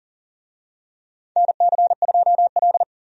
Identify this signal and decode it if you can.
Morse code, heard as a keyed tone, about 40 wpm, 700 Hz. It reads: NC2L
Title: USA Call Signs - 4 Characters - 40wpm